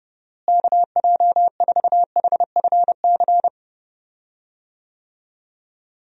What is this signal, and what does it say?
Morse code, heard as a keyed tone, about 30 wpm, 700 Hz. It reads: KJ4HFC
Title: USA Call Signs - 6 Characters - Repeated 3x - 30wpm